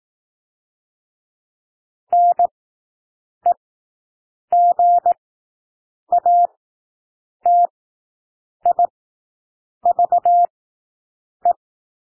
Morse code, heard as a keyed tone, about 18 wpm, spaced as for 7 wpm, 700 Hz. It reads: NEGATIVE
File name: SOUND WORDS 18 wpm